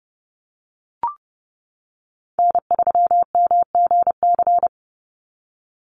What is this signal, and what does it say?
Morse code, heard as a keyed tone, about 30 wpm, 700 Hz. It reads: N3MGC
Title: USA Call Signs 30wpm